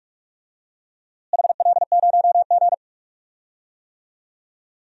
Morse code, heard as a keyed tone, about 46 wpm, 700 Hz. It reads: HL0G
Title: DXCC-100 Call Signs - 4 Characters - Repeated 3x - 45wpm